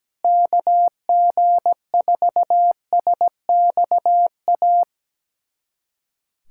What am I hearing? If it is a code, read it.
Morse code, heard as a keyed tone, about 17 wpm, 700 Hz. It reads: KG4SXA
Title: USA Callsign DE Callsign 17wpm